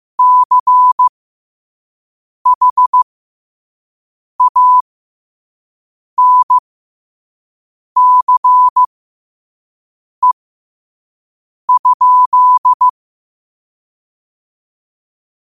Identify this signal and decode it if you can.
Morse code, heard as a keyed tone, about 15 wpm, spaced as for 5 wpm, 1000 Hz. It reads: CHANCE?
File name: 5wpm